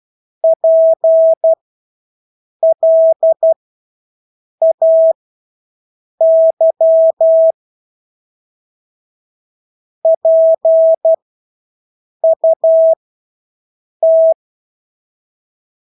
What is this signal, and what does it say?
Morse code, heard as a keyed tone, about 12 wpm, spaced as for 6 wpm, 650 Hz. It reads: PLAY PUT
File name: Engelske ord 12-6 wpm